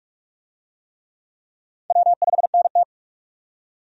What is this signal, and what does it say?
Morse code, heard as a keyed tone, about 45 wpm, 700 Hz. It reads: W5NT
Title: USA Call Signs - 4 Characters - 45wpm